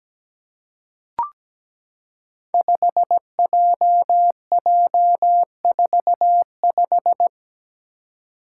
Morse code, heard as a keyed tone, about 17 wpm, 700 Hz. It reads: 5JJ45
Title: Five Challenging Letter-Number Combinations 17wpm